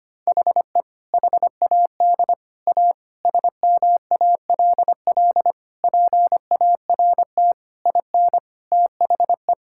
Morse code, heard as a keyed tone, about 25 wpm, 700 Hz. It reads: HE HAD A SMALL PART IN THE
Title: Sentences from Top 1000 Words - Review Part 5 - 25wpm